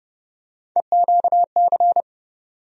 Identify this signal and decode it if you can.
Morse code, heard as a keyed tone, about 30 wpm, 700 Hz. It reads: EQC
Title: USA Call Sign Postfixes - 3 Letters 30wpm